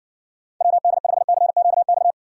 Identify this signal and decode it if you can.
Morse code, heard as a keyed tone, about 60 wpm, 700 Hz. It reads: WD5CQX